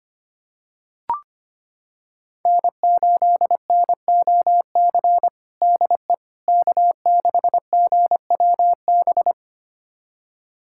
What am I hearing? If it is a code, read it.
Morse code, heard as a keyed tone, about 25 wpm, 700 Hz. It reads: N8NOC DE K6GWB
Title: USA Callsign DE Callsign - 5 Characters - 25wpm